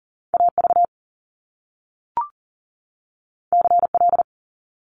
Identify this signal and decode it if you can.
Morse code, heard as a keyed tone, about 40 wpm, 700 Hz. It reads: AV CL
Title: Top Two Letter Combinations in Words 40wpm